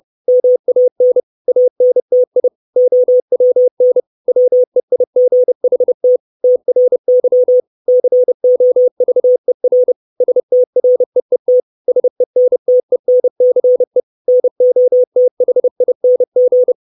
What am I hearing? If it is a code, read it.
Morse code, heard as a keyed tone, about 30 wpm, 500 Hz. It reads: MAN ANTI OWN WEIGHT TRY COVER STREET SENTENCE NOTHING
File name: 632 Words 30WPM